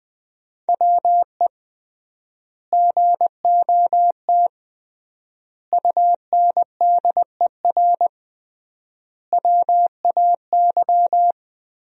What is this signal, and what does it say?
Morse code, heard as a keyed tone, about 20 wpm, 700 Hz. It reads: WE GOT UNDER WAY
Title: 3x Word-Spacing - Sentences from Top 300 Words - 20wpm